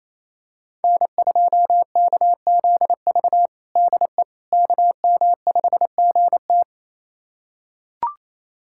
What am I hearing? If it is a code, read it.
Morse code, heard as a keyed tone, about 28 wpm, 700 Hz. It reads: N2KZV DE KM5GT
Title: USA Callsign DE Callsign - 5 Characters - 28wpm